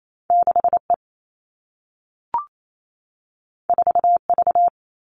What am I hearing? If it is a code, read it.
Morse code, heard as a keyed tone, about 28 wpm, 700 Hz. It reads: 6E 4V
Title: All Cut Numbers - 2-Digit 28wpm